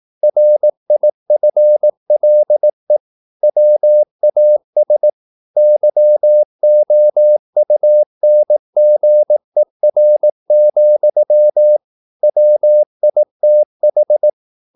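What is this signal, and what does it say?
Morse code, heard as a keyed tone, about 18 wpm, 600 Hz. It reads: RIFLE WAS YOUNGER, WITH
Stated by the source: Morse practice at 18 wpm, English text, from CW med Gnister.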